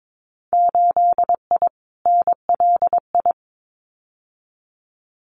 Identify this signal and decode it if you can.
Morse code, heard as a keyed tone, about 22 wpm, 700 Hz. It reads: 8I NLI
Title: Winter Field Day Exchanges Encore 22wpm